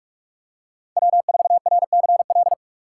Morse code, heard as a keyed tone, about 39 wpm, 700 Hz. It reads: W4RCL